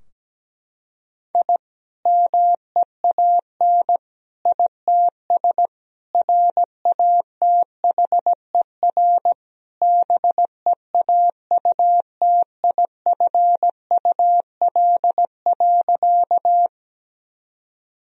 Morse code, heard as a keyed tone, about 17 wpm, 700 Hz. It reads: I MEAN ITS RATHER BEAUTIFUL.
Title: Sentences from Top 1000 Words - Review Part 4 - 17wpm